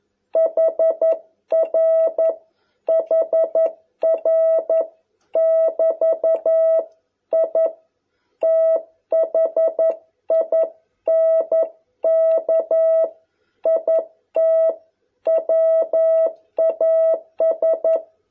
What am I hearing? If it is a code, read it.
Morse code, heard as a keyed tone, about 11 wpm, 660 Hz. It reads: HR HR <BT> I THINK IT WAS